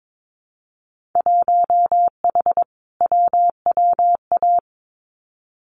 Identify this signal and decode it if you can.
Morse code, heard as a keyed tone, about 22 wpm, 700 Hz. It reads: 1H WWA
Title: Winter Field Day Exchanges Encore 22wpm